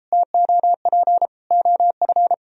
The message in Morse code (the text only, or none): TOP OF